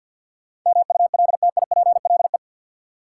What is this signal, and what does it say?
Morse code, heard as a keyed tone, about 48 wpm, 700 Hz. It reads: MULTIPLE
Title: Sets of 5 Words - New Word Drill - 50wpm